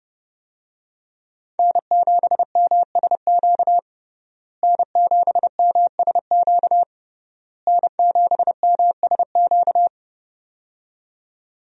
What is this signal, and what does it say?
Morse code, heard as a keyed tone, about 32 wpm, 700 Hz. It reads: N7MSQ N7MSQ N7MSQ
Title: USA Call Signs - 5 Characters - Repeated 3x - 30wpm